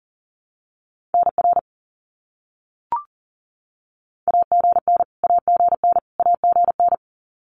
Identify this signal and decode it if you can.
Morse code, heard as a keyed tone, about 40 wpm, 700 Hz. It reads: NR AGN AGN AGN
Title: Abbreviations Repeated 3x - 40wpm